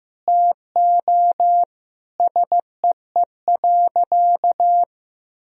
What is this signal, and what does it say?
Morse code, heard as a keyed tone, about 15 wpm, 700 Hz. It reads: TO SEE.
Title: Sentences from Top 200 Words and Abbreviations 15wpm